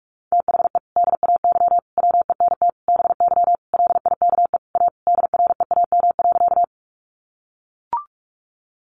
Morse code, heard as a keyed tone, about 44 wpm, 700 Hz. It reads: THE DAY WENT BY LIKE A DREAM.
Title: Sentences from Top 1500 Words 45wpm